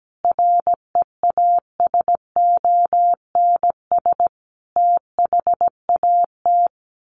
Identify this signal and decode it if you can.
Morse code, heard as a keyed tone, about 17 wpm, 700 Hz. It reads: REASONS THAT
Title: Sets of 5 Words Encore 17wpm